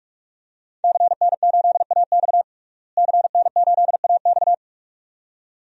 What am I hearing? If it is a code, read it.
Morse code, heard as a keyed tone, about 43 wpm, 700 Hz. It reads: CN8AX CN8AX
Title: DXCC-100 Call Signs - 5 Characters - Repeated 2x - 45wpm